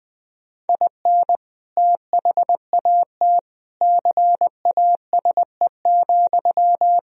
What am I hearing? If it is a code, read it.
Morse code, heard as a keyed tone, about 21 wpm, 700 Hz. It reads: IN THAT CASE,
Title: Sentences from Top 1000 Words - Review Part 2 - 20wpm